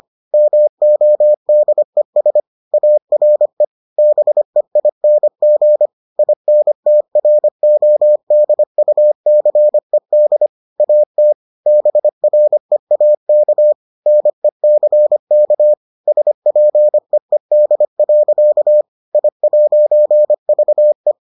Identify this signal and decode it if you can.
Morse code, heard as a keyed tone, about 25 wpm, 600 Hz. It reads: MODES ARE BEING INTRODUCED AT BREAK NECK SPEED. I'VE